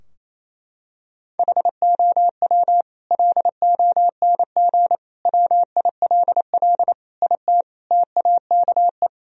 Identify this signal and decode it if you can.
Morse code, heard as a keyed tone, about 28 wpm, 700 Hz. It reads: HOW LONG WILL IT TAKE
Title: Sentences from Top 1000 Words - Review Part 2 - 28wpm